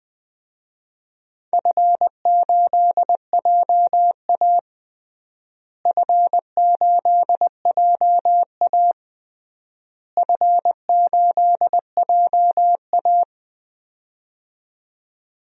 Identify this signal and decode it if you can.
Morse code, heard as a keyed tone, about 20 wpm, 700 Hz. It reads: F8JA F8JA F8JA
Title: DXCC-100 Call Signs - 4 Characters - Repeated 3x - 20wpm